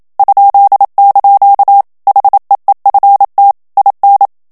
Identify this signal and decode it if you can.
Morse code, heard as a keyed tone, about 27 wpm, 800 Hz. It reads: ?) HEEFT IN
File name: HOLLANDAIS texte clair 1 vitesse 26 mots minute